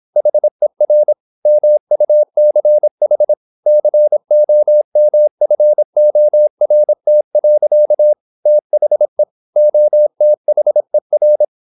Morse code, heard as a keyed tone, about 26 wpm, 600 Hz. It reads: HER MUCH COMFORT. THE OTHER
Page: Never 26wpm | CW med Gnister